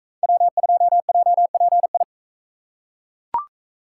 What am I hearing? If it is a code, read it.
Morse code, heard as a keyed tone, about 39 wpm, 700 Hz. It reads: W2JPI